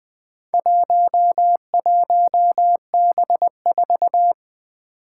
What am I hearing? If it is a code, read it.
Morse code, heard as a keyed tone, about 20 wpm, 700 Hz. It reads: 11B4
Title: Four Challenging Letter-Number Combinations 20wpm